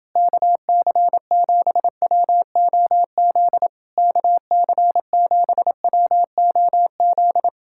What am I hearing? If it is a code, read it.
Morse code, heard as a keyed tone, about 30 wpm, 700 Hz. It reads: KC7WOZ KC7WOZ